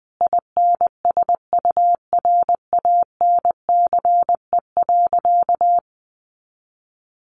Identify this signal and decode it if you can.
Morse code, heard as a keyed tone, about 20 wpm, 700 Hz. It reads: INSURANCE.
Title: A Sound Of Thunder 20wpm